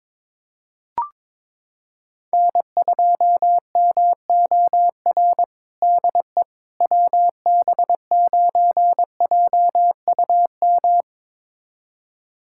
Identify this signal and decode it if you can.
Morse code, heard as a keyed tone, about 22 wpm, 700 Hz. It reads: N2MOR DE WB9JUM
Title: USA Callsign DE Callsign 22wpm